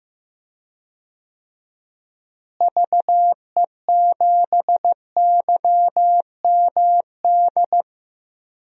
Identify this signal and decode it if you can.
Morse code, heard as a keyed tone, about 15 wpm, 700 Hz. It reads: VE7YMD
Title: DXCC-100 Call Signs - 6 Characters - Repeated 2x - 15wpm